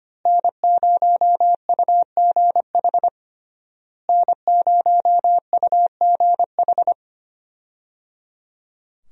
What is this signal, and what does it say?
Morse code, heard as a keyed tone, about 25 wpm, 700 Hz. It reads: N0UGH N0UGH
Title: USA Call Signs - 5 Characters - Repeated 3x - 25wpm